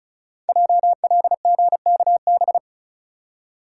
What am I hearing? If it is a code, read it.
Morse code, heard as a keyed tone, about 36 wpm, 700 Hz. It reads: JLGKB
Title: Code Groups - 5 Letters - 35wpm